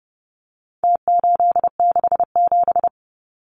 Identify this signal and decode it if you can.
Morse code, heard as a keyed tone, about 30 wpm, 700 Hz. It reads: T867
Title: Numbers - Contest Style - 4-Digit 30wpm